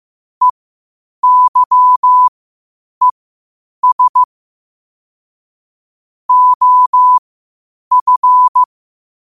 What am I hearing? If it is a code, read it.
Morse code, heard as a keyed tone, about 15 wpm, spaced as for 8 wpm, 1000 Hz. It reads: EYES OF
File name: qotdfiles / 8wpm